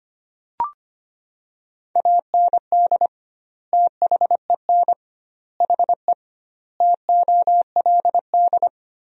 Morse code, heard as a keyed tone, about 25 wpm, 700 Hz. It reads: AND THEN HE TOLD
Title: 2x Word-Spacing - Sentences from Top 400 Words - 25wpm